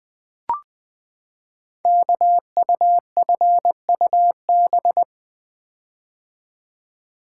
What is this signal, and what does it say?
Morse code, heard as a keyed tone, about 20 wpm, 700 Hz. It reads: KUFUB
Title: Code Groups - 5 Letters - 20wpm